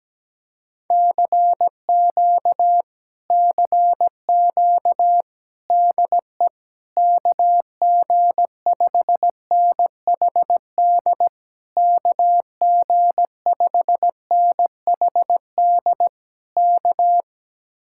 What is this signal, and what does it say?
Morse code, heard as a keyed tone, about 17 wpm, 700 Hz. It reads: CQ CQ DE KG5NHD KG5NHD K
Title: Speed-Racing Common QSO Phrases Encore - 17wpm